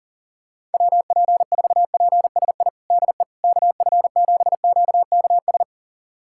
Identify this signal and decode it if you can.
Morse code, heard as a keyed tone, about 40 wpm, 700 Hz. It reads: WP4PSI DE KF7QKS
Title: USA Callsign DE Callsign - 6 Characters - 40wpm